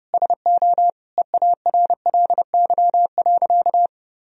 Morse code, heard as a keyed tone, about 32 wpm, 700 Hz. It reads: SO EARLY.
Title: Sentences from Top 1000 Words - Review Part 2 - 30wpm